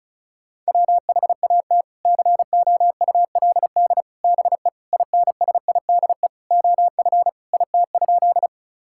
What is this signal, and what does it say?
Morse code, heard as a keyed tone, about 36 wpm, 700 Hz. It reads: WHAT COULD BE INSIDE OF IT?
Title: Sentences from Top 1000 Words - Review Part 6 - 35wpm